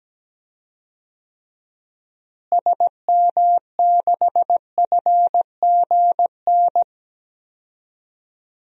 Morse code, heard as a keyed tone, about 17 wpm, 700 Hz. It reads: SM6FGN
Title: DXCC-100 Call Signs - 6 Characters - Repeated 3x - 17wpm